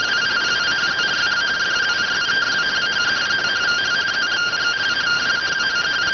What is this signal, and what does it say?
Two-tone teleprinter signal, about 170 Hz shift.